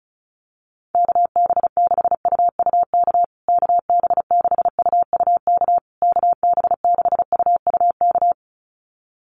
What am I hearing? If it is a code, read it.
Morse code, heard as a keyed tone, about 34 wpm, 700 Hz. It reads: KB6UUK KB6UUK KB6UUK
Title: USA Call Signs Repeated 3x 35wpm